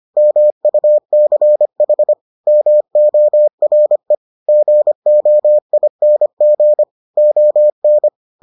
Morse code, heard as a keyed tone, about 25 wpm, 600 Hz. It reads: MUCH MORE GOING ON